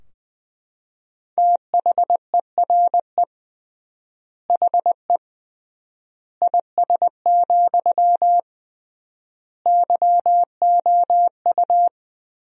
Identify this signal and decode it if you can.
Morse code, heard as a keyed tone, about 21 wpm, 700 Hz. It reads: THERE HE IS, YOU
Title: 3x Word-Spacing - Sentences from Top 100 Words - 20wpm